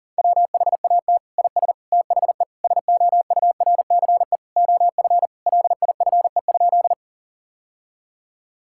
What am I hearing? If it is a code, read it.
Morse code, heard as a keyed tone, about 40 wpm, 700 Hz. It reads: WHAT IS THE SOURCE OF LIFE?
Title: Sentences from Top 1000 Words - Review Part 5 - 40wpm